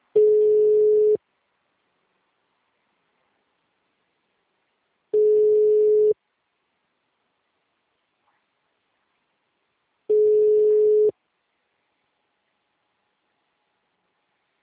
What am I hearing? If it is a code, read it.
Ringback tone (Europe, 425 Hz).
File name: out- -2030- - - .55667